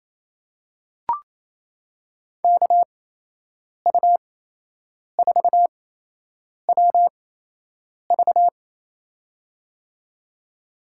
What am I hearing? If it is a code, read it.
Morse code, heard as a keyed tone, about 27 wpm, spaced as for 8 wpm, 700 Hz. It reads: KU4WV
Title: USA Call Signs - 5 Characters - 8x Character Spacing - 28wpm